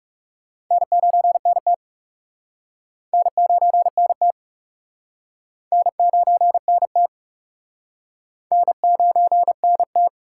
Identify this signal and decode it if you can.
Morse code, heard as a keyed tone, about 37 wpm, 700 Hz. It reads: N9NT N9NT N9NT N9NT
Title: Speed Racing - USA 1x2 Call Signs 30wpm